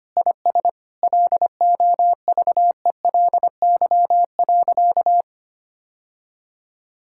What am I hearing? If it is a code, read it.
Morse code, heard as a keyed tone, about 25 wpm, 700 Hz. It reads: IS LOVELY.
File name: practice / 25wpm / Sentences with 3-Word Phrases - 25wpm